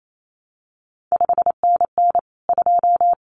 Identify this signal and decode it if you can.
Morse code, heard as a keyed tone, about 28 wpm, 700 Hz. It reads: 5NN 2
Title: CQ WW DX Contest Exchange - 28wpm